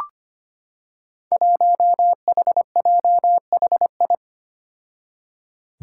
Morse code, heard as a keyed tone, about 25 wpm, 700 Hz. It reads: 1HJHI
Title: Five Challenging Letter-Number Combinations 25wpm